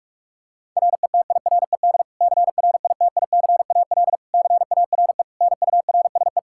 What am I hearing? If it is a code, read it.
Morse code, heard as a keyed tone, about 46 wpm, 700 Hz. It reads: RETIRED CRITICAL CARE NURSE
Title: Practice QSO - Exchange 3 - More Info - 45wpm